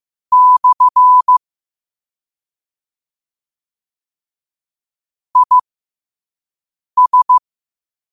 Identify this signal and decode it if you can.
Morse code, heard as a keyed tone, about 15 wpm, spaced as for 5 wpm, 1000 Hz. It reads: / IS
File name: qotdfiles / 5wpm